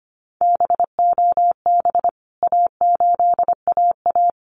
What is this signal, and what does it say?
Morse code, heard as a keyed tone, about 25 wpm, 700 Hz. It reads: BOB A8AA
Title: CWT Exchange 25wpm